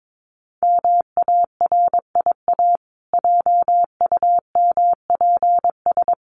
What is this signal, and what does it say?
Morse code, heard as a keyed tone, about 22 wpm, 700 Hz. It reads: MARIA JUMPS
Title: Easy Phrases 22wpm